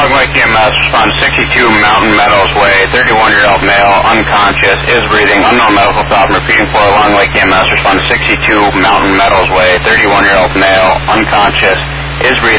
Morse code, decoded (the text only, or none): none